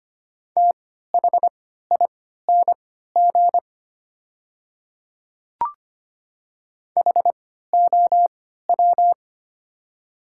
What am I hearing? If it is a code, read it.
Morse code, heard as a keyed tone, about 25 wpm, spaced as for 14 wpm, 700 Hz. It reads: THING HOW
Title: 3x Character Spacing - Top 200 Words - 25wpm